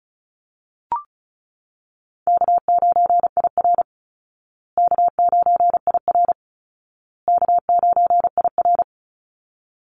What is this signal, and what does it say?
Morse code, heard as a keyed tone, about 34 wpm, 700 Hz. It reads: K9IR K9IR K9IR
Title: Easy Call Signs Repeated 3x - 35wpm